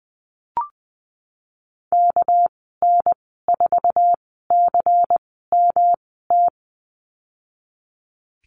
Morse code, heard as a keyed tone, about 20 wpm, spaced as for 15 wpm, 700 Hz. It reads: KN4CMT
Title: 213 Lesson 34g - Number 8 - Call Signs 20wpm